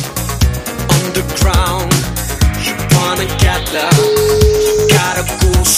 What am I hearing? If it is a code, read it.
Ringback tone (Europe, 425 Hz), with noise.